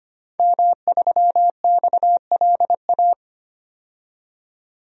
Morse code, heard as a keyed tone, about 25 wpm, 700 Hz. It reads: M3XLA
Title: DXCC-100 Call Signs - 5 Characters - 25wpm